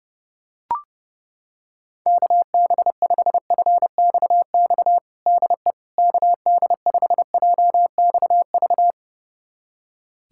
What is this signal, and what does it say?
Morse code, heard as a keyed tone, about 30 wpm, 700 Hz. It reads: KB5FXX DE KD5JXV
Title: USA Callsign DE Callsign - 6 Characters - 30wpm